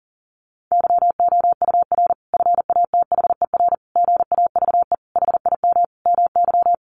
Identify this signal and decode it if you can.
Morse code, heard as a keyed tone, about 40 wpm, 700 Hz. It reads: YOUR FATHER GAVE HIM MY
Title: Sentences from Top 1400 Words 40wpm